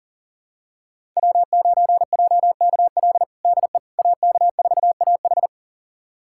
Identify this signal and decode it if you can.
Morse code, heard as a keyed tone, about 40 wpm, 700 Hz. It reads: W9JKL DE AK4AH
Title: USA Callsign DE Callsign 40wpm